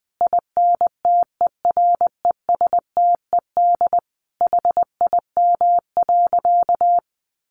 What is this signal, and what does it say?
Morse code, heard as a keyed tone, about 20 wpm, 700 Hz. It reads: INTERESTED HIM.
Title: Sentences from Top 1500 Words 20wpm